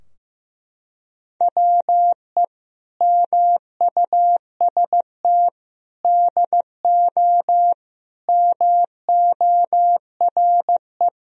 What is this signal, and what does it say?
Morse code, heard as a keyed tone, about 15 wpm, 700 Hz. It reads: WE MUST DO MORE
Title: Sentences from Top 1000 Words - Review Part 3 - 15wpm